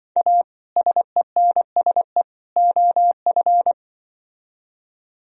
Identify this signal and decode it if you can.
Morse code, heard as a keyed tone, about 24 wpm, 700 Hz. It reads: A SENSE OF
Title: Speed-Racing - Sets of 3 Words 20wpm